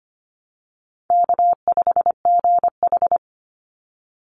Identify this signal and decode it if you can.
Morse code, heard as a keyed tone, about 25 wpm, 700 Hz. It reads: K5GH
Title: Easy Call Signs - 25wpm